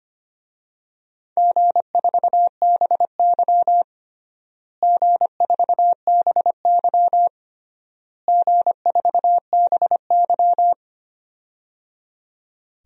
Morse code, heard as a keyed tone, about 25 wpm, 700 Hz. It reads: G4BY G4BY G4BY
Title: DXCC-100 Call Signs - 4 Characters - Repeated 3x - 25wpm